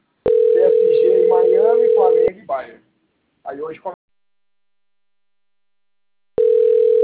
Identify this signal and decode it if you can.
Ringback tone (North America), 440 + 480 Hz, with speech.
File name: rg-1600-unknown- - - .85888